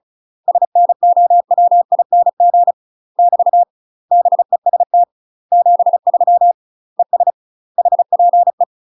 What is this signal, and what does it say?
Morse code, heard as a keyed tone, about 33 wpm, 700 Hz. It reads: SNOWING <BT> BEST 73 ES HPE
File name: UK MASTER QSOs TRAINING / 35WPM / UK MASTER QSOs 0023